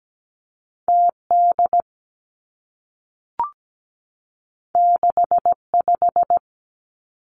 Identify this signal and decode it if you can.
Morse code, heard as a keyed tone, about 17 wpm, 700 Hz. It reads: TD 65
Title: Two Challenging Letter-Number Combinations 17wpm